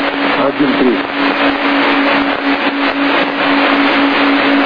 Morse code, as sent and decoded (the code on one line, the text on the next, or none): none